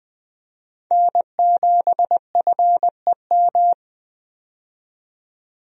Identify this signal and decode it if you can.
Morse code, heard as a keyed tone, about 20 wpm, 700 Hz. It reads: N7FEM
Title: USA Call Signs - 5 Characters - 20wpm